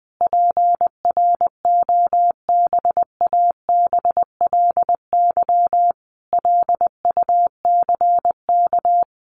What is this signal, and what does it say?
Morse code, heard as a keyed tone, about 20 wpm, 700 Hz. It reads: PROBABLY LUCK
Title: Sentences from Top 1500 Words 20wpm